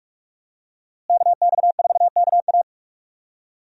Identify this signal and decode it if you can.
Morse code, heard as a keyed tone, about 45 wpm, 700 Hz. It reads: KX4KA
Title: USA Call Signs - 5 Characters - Repeated 2x - 45wpm